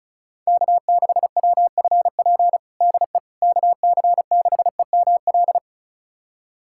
Morse code, heard as a keyed tone, about 36 wpm, 700 Hz. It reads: K6WFP DE KC6EML
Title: USA Callsign DE Callsign 35wpm